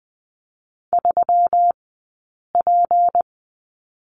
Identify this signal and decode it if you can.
Morse code, heard as a keyed tone, about 20 wpm, spaced as for 8 wpm, 700 Hz. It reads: 3P
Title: 197 Lesson 32d - Letter k - Two Characters 20wpm